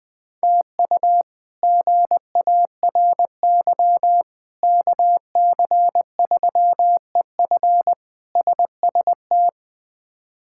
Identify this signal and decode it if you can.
Morse code, heard as a keyed tone, about 21 wpm, 700 Hz. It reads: TU GARY KC3EF SST
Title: K1USN SST QSO Practice - 20wpm